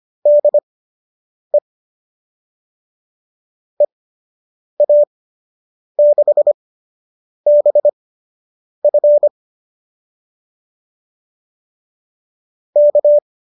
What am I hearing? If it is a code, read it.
Morse code, heard as a keyed tone, about 25 wpm, spaced as for 8 wpm, 590 Hz. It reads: DE EA6BF K